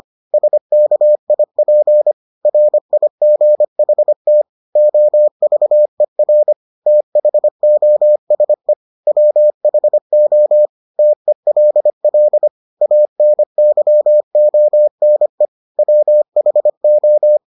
Morse code, heard as a keyed tone, about 25 wpm, 600 Hz. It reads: SKIP RIGHT OVER THOSE WHO TELL ANYONE WHO